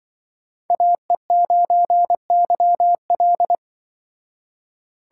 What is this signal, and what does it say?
Morse code, heard as a keyed tone, about 24 wpm, 700 Hz. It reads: AE9YL
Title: Speed Racing - USA 2x2 Call Signs 20wpm